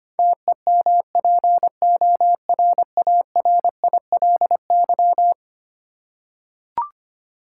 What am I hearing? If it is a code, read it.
Morse code, heard as a keyed tone, about 25 wpm, 700 Hz. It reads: TEMPORARILY
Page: morse-code-ninja-practice-25wpm